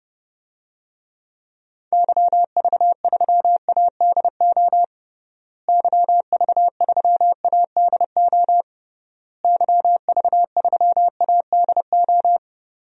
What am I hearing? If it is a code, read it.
Morse code, heard as a keyed tone, about 30 wpm, 700 Hz. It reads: YV3ADO YV3ADO YV3ADO
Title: DXCC-100 Call Signs - 6 Characters - Repeated 3x - 30wpm